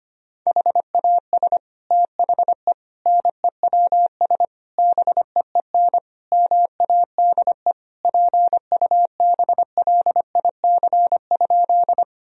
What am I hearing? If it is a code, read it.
Morse code, heard as a keyed tone, about 25 wpm, 700 Hz. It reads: HAS THE NEWS BEEN MADE PUBLIC?
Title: Sentences from Top 1000 Words - Review Part 3 - 25wpm